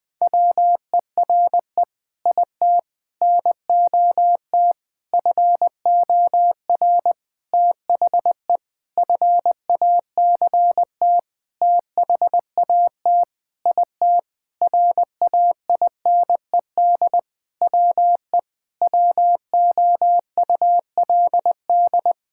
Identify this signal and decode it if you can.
Morse code, heard as a keyed tone, about 20 wpm, 700 Hz. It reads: WERE IT NOT FOR THE FACT THAT IT RAINED WE WOULD
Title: Sentences with 7-Word Phrases - 20wpm